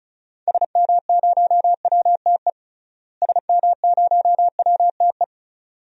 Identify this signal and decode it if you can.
Morse code, heard as a keyed tone, about 34 wpm, 700 Hz. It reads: SM0WTE SM0WTE
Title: DXCC-100 Call Signs - 6 Characters - Repeated 2x - 35wpm